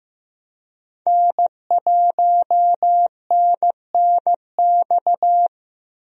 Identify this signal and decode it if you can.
Morse code, heard as a keyed tone, about 15 wpm, 700 Hz. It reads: N1NNX
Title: USA Call Signs - 5 Characters - Repeated 3x - 15wpm